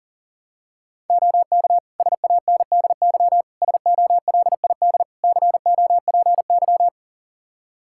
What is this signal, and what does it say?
Morse code, heard as a keyed tone, about 40 wpm, 700 Hz. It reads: OK SANDY SOLID COPY
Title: Practice QSO - Exchange 3 - More Info - 40wpm